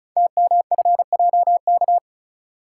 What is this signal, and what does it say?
Morse code, heard as a keyed tone, about 36 wpm, 700 Hz. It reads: TMFJK
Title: Code Groups - 5 Letter:Numbers - 35wpm